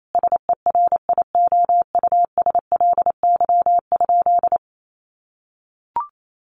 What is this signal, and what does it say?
Morse code, heard as a keyed tone, about 28 wpm, 700 Hz. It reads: SERIOUSLY?
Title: Sentences from Top 1200 Words 28wpm